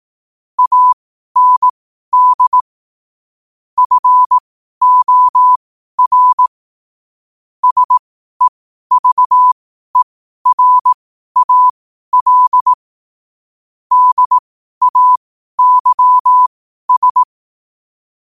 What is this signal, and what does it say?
Morse code, heard as a keyed tone, about 18 wpm, spaced as for 12 wpm, 1000 Hz. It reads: AND FOR SEVERAL DAYS
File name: qotdfiles / 12wpm